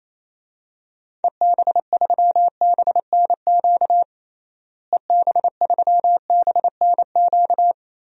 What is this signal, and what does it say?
Morse code, heard as a keyed tone, about 28 wpm, 700 Hz. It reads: EB3BNQ EB3BNQ
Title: DXCC-100 Call Signs - 6 Characters - Repeated 2x - 28wpm